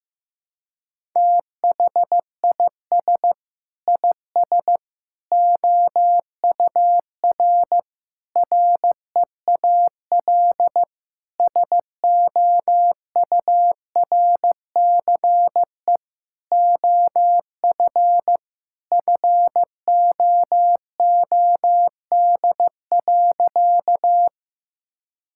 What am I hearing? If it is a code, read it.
Morse code, heard as a keyed tone, about 15 wpm, 700 Hz. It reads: THIS IS OUR REAL SOURCE OF FOOD.
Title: Sentences from Top 1000 Words - Review Part 5 - 15wpm